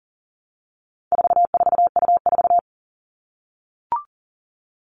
Morse code, heard as a keyed tone, about 40 wpm, 700 Hz. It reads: 44U4
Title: All Cut Numbers - 4-Digit 40wpm